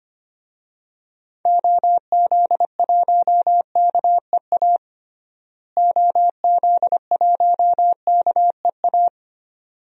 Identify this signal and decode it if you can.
Morse code, heard as a keyed tone, about 25 wpm, 700 Hz. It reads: OZ1KEA OZ1KEA
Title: DXCC-100 Call Signs - 6 Characters - Repeated 2x - 25wpm